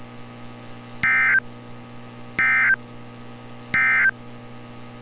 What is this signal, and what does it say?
Emergency Alert System (SAME) tones, heard as harsh buzzing bursts. It is heard from 0.9 to 4.2 s.
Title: NOAA Weather Radio - Altoona, PA